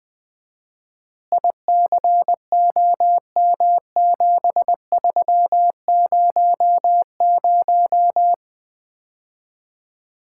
Morse code, heard as a keyed tone, about 20 wpm, 700 Hz. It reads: ICOM7300
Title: Common QSO Phrases - 4x Word-Spacing - 20wpm